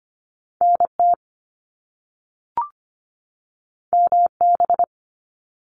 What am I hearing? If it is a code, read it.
Morse code, heard as a keyed tone, about 25 wpm, 700 Hz. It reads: NT MB
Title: Canadian Provinces and Territories 25wpm